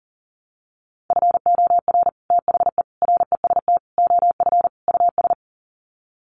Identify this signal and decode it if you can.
Morse code, heard as a keyed tone, about 40 wpm, 700 Hz. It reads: FOR THE REST OF US
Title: Sets of 5 Words Encore 40wpm